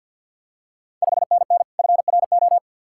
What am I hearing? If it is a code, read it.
Morse code, heard as a keyed tone, about 53 wpm, 700 Hz. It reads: 5NN FRO
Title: Ontario QSO Party Exchange 50wpm